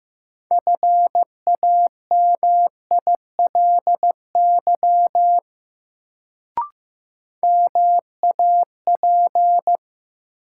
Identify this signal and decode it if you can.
Morse code, heard as a keyed tone, about 15 wpm, 700 Hz. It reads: FAMILY MAP
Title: Sentences from Top 400 Words - New Word Drill - 15wpm